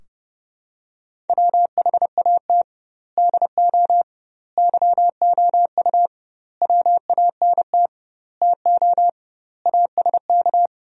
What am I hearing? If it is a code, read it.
Morse code, heard as a keyed tone, about 30 wpm, 700 Hz. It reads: WHAT DO YOU WANT TO ASK
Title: 2x Word-Spacing - Sentences from Top 200 Words - 30wpm